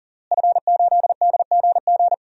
Morse code, heard as a keyed tone, about 40 wpm, 700 Hz. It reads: F8DGG
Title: DXCC-100 Call Signs - 5 Characters - Repeated 3x - 40wpm